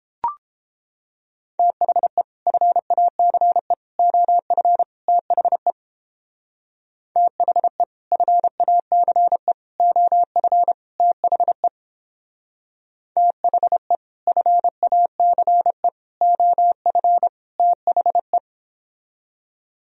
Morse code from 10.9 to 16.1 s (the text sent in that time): THE THE FACE